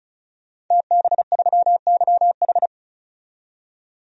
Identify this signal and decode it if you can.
Morse code, heard as a keyed tone, about 34 wpm, 700 Hz. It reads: TB3YH
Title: DXCC-100 Call Signs - 5 Characters - 35wpm